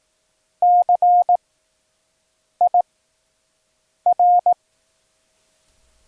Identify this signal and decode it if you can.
Morse code, heard as a keyed tone, about 18 wpm, 710 Hz. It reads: C I R
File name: acs / Instrument Rating / 17 Circling Approach